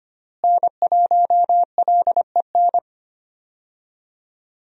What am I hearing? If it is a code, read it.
Morse code, heard as a keyed tone, about 25 wpm, 700 Hz. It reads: N1LEN
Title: USA Call Signs - 5 Characters - Repeated 3x - 25wpm